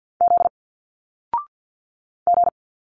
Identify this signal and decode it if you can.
Morse code, heard as a keyed tone, about 49 wpm, 700 Hz. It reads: Z G
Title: Single Letters 50wpm